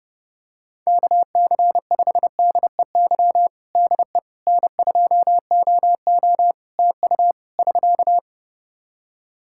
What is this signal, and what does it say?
Morse code, heard as a keyed tone, about 30 wpm, 700 Hz. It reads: KC5DEY DE N2OO TU <SK>
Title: K1USN SST QSO Practice 30wpm